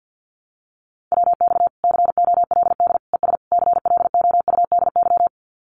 Morse code, heard as a keyed tone, about 50 wpm, 700 Hz. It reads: WX COLD ES CLOUDY
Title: Common QSO Phrases Encore 50wpm